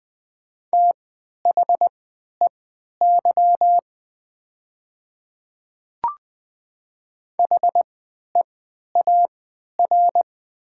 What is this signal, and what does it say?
Morse code, heard as a keyed tone, about 20 wpm, spaced as for 11 wpm, 700 Hz. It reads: THEY HEAR
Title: 3x Character Spacing - Top 300 Words - 20wpm